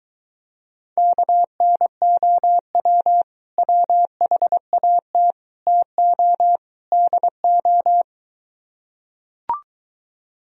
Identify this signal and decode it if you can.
Morse code, heard as a keyed tone, about 23 wpm, 700 Hz. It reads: KNOW WHAT TO DO